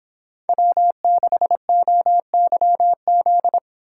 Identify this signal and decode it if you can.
Morse code, heard as a keyed tone, about 30 wpm, 700 Hz. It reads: W6OYZ